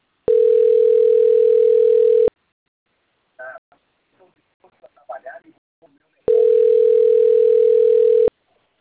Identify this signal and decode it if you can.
Ringback tone (North America), 440 + 480 Hz, with speech.
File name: rg-1600-unknown- - - .8252